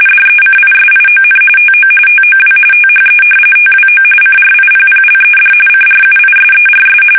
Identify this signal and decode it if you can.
RTTY, about 45.45 baud. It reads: DECISION TO TOW THE TWO CRIPPLED CRUISERS
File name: fsk850 / battleshipnj